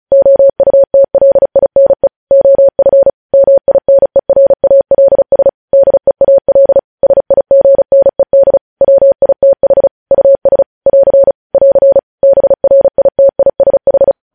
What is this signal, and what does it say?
Morse code, heard as a keyed tone, about 36 wpm, 550 Hz. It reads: OUTLINE OF MINERALS DEAL SIGNED WITH US <AR> <AR> BRITISH
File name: podcast / News headlines-1x-35x35